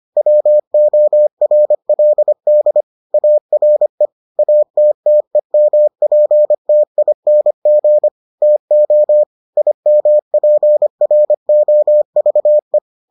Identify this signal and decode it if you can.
Morse code, heard as a keyed tone, about 25 wpm, 600 Hz. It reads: WORLD ARE ATTEMPTING TO IMPROVE